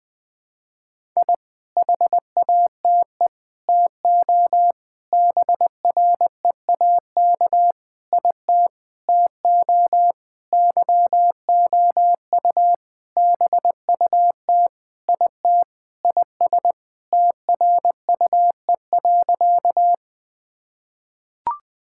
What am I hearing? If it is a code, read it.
Morse code, heard as a keyed tone, about 20 wpm, 700 Hz. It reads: I HATE TO BREAK IT TO YOU BUT IT IS TRUE.
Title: Sentences with 8-Word Phrases - Spoken Beginning - 20wpm